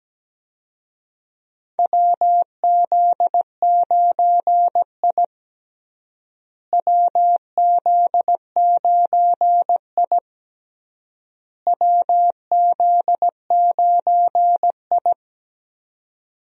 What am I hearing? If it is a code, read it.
Morse code, heard as a keyed tone, about 17 wpm, 700 Hz. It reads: WZ9I WZ9I WZ9I
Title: USA Call Signs - 4 Characters - Repeated 3x - 17wpm